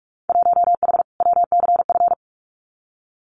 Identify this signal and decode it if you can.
Morse code, heard as a keyed tone, about 46 wpm, 700 Hz. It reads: 1H WCF
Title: Winter Field Day Exchanges Encore 45wpm